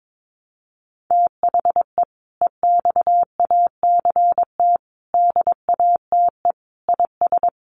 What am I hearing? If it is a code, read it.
Morse code, heard as a keyed tone, about 22 wpm, 700 Hz. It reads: THE EXACT DATE IS
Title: Sentences from Top 1300 Words 22wpm